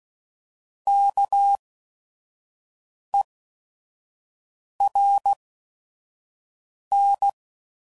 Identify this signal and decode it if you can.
Morse code, heard as a keyed tone, about 16 wpm, spaced as for 5 wpm, 800 Hz. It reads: KERN